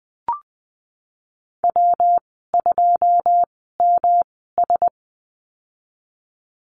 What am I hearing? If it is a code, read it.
Morse code, heard as a keyed tone, about 20 wpm, spaced as for 15 wpm, 700 Hz. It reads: W2MS
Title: 122 Lesson 20g - Number 3 - Call Signs 20wpm